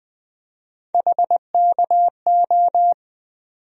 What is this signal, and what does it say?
Morse code, heard as a keyed tone, about 20 wpm, 700 Hz. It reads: HKO
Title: USA Call Sign Postfixes - 3 Letters 20wpm